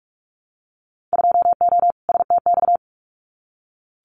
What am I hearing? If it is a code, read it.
Morse code, heard as a keyed tone, about 46 wpm, 700 Hz. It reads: 2O STX
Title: Winter Field Day Exchanges 45wpm